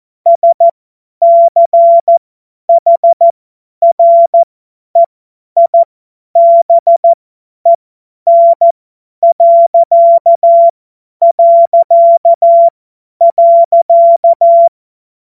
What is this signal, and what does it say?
Morse code, heard as a keyed tone, about 14 wpm, spaced as for 10 wpm, 680 Hz. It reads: SCHREIBEN...
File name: Morsekurs / Witze / Witz 27. Dezember 2022 50 F70